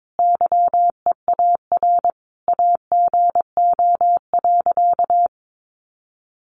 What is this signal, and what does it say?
Morse code, heard as a keyed tone, about 22 wpm, 700 Hz. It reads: YEAR AGO.
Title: Sentences from Top 1300 Words 22wpm